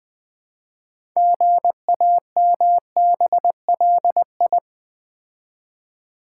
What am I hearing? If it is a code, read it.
Morse code, heard as a keyed tone, about 20 wpm, 700 Hz. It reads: GAMBLI
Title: Build A Word - 2 Letters At A Time - 8-Letter Words - 20wpm